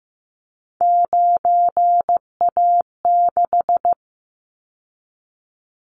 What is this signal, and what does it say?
Morse code, heard as a keyed tone, about 15 wpm, 700 Hz. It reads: 9A6
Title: Callsign Prefixes for Top 25 DXCC - 15wpm